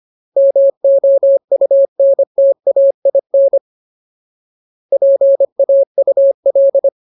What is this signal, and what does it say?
Morse code, heard as a keyed tone, about 25 wpm, 550 Hz. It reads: MOUNTAIN PAUL
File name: Ultimate Word List / 25WPM Ultimate